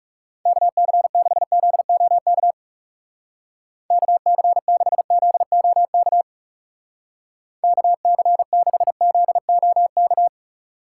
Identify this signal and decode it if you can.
Morse code, heard as a keyed tone, about 37 wpm, 700 Hz. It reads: KC6ZOK KC6ZOK KC6ZOK
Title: Speed Racing - USA 2x3 Call Signs 30wpm